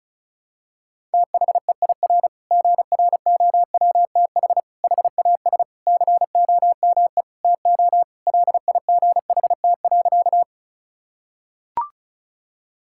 Morse code, heard as a keyed tone, about 36 wpm, 700 Hz. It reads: THEIR GROWTH HAS COME TO LIGHT.
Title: Sentences from Top 1000 Words - Review Part 5 - 35wpm